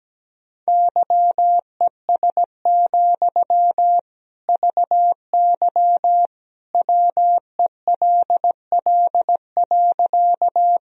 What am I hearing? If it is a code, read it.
Morse code, heard as a keyed tone, about 17 wpm, 700 Hz. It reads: YES, VY WELL.
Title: Sentences from Top 500 Words and Abbreviations 17wpm